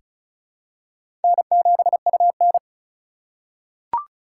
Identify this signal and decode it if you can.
Morse code, heard as a keyed tone, about 36 wpm, 700 Hz. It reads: N7UN
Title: USA Call Signs - 3x Character Spacing - 35wpm